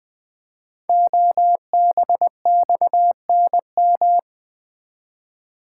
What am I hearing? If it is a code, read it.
Morse code, heard as a keyed tone, about 20 wpm, 700 Hz. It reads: OBXNM
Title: Code Groups - 5 Letters - 20wpm